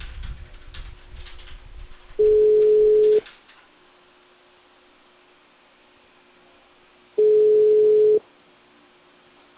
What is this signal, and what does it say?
Ringback tone (Europe, 425 Hz).